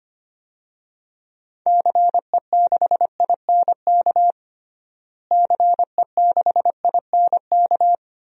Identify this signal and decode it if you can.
Morse code, heard as a keyed tone, about 25 wpm, 700 Hz. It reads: CE6INK CE6INK
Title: DXCC-100 Call Signs - 6 Characters - Repeated 2x - 25wpm